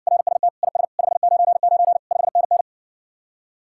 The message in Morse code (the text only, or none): RST IS 599 5NN